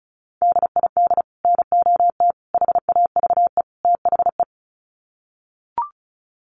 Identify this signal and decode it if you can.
Morse code, heard as a keyed tone, about 34 wpm, 700 Hz. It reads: DID NOT HAVE THE
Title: Sets of 4 Words 35wpm